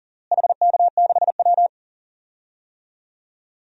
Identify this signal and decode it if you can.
Morse code, heard as a keyed tone, about 40 wpm, 700 Hz. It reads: HK6W
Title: DXCC-100 Call Signs - 4 Characters - Repeated 2x - 40wpm